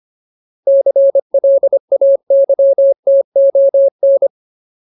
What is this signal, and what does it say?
Morse code, heard as a keyed tone, about 25 wpm, 550 Hz. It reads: CLAYTON
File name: Ultimate Word List / 25WPM Ultimate